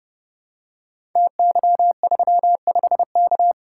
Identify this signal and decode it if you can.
Morse code, heard as a keyed tone, about 30 wpm, 700 Hz. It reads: TY35K
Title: Code Groups - 5 Letter:Numbers - 30wpm